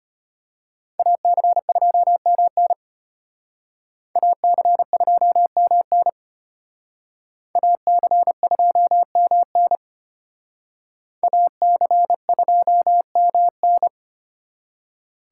Morse code, heard as a keyed tone, about 32 wpm, 700 Hz. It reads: AC2MN AC2MN AC2MN AC2MN